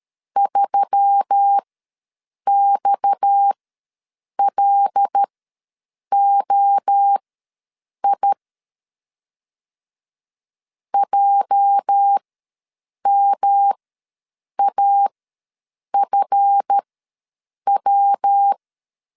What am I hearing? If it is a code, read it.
Morse code, heard as a keyed tone, about 13 wpm, spaced as for 7 wpm, 800 Hz. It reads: 3XLOI JMAFW
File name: LEZIONI / 8 YJ45 / 8 YJ4 ws 12wpm 3249